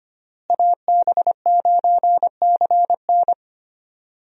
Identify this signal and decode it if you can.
Morse code, heard as a keyed tone, about 25 wpm, 700 Hz. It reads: AB9CN
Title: USA Call Signs - 5 Characters - Repeated 3x - 25wpm